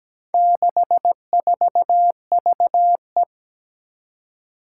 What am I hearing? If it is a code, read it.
Morse code, heard as a keyed tone, about 17 wpm, 700 Hz. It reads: 64VE
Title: Four Challenging Letter-Number Combinations 17wpm